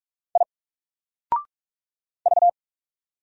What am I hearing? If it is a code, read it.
Morse code, heard as a keyed tone, about 46 wpm, 700 Hz. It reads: I V
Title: Single Letter-Number 45wpm